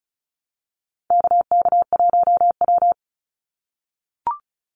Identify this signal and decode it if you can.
Morse code, heard as a keyed tone, about 34 wpm, 700 Hz. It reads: KK1W
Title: NA SOTA Activator Callsigns 35wpm